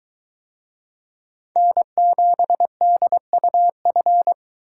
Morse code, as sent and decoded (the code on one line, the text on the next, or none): -. --... -.. ..- ..-.
N7DUF